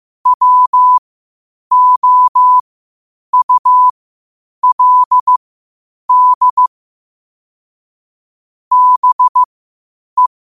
Morse code, heard as a keyed tone, about 15 wpm, spaced as for 8 wpm, 1000 Hz. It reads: WOULD BE
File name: qotdfiles / 8wpm